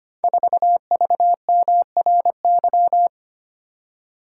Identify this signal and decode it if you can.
Morse code, heard as a keyed tone, about 25 wpm, 700 Hz. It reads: 4VMRY
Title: Code Groups - 5 Letter:Numbers - 25wpm